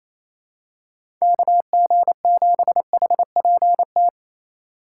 Morse code, heard as a keyed tone, about 28 wpm, 700 Hz. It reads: KG7HPT
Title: USA Call Signs - 3x Character Spacing - 28wpm